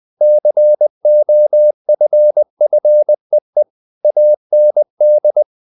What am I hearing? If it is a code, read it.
Morse code, heard as a keyed tone, about 20 wpm, 600 Hz. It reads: COFFEE AND
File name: the old man and the sea part 2 20wpm